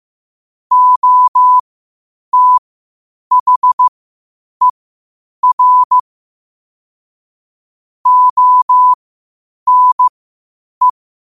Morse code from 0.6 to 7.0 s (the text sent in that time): OTHER